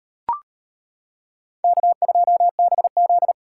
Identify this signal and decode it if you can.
Morse code, heard as a keyed tone, about 38 wpm, 700 Hz. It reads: K2BZ